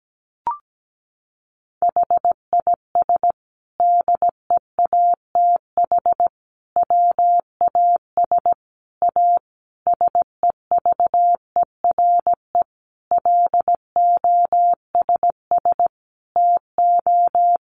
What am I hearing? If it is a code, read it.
Morse code, heard as a keyed tone, about 17 wpm, 700 Hz. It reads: HIS DEATH WAS A SEVERE LOSS TO
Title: Sentences from Top 1400 Words 17wpm